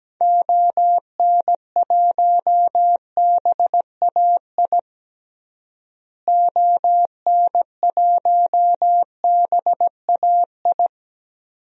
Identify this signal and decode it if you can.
Morse code, heard as a keyed tone, about 17 wpm, 700 Hz. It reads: ON1BAI ON1BAI
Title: DXCC-100 Call Signs - 6 Characters - Repeated 2x - 17wpm